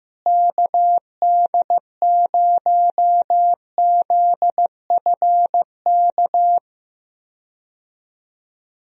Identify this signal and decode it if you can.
Morse code, heard as a keyed tone, about 15 wpm, 700 Hz. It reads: KD0ZFK
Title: USA Call Signs - 6 Characters - Repeated 3x - 15wpm